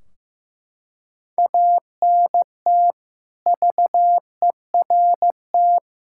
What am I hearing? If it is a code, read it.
Morse code, heard as a keyed tone, about 15 wpm, 700 Hz. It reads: ANT VERT
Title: Practice QSO - Exchange 2 - Info - 15wpm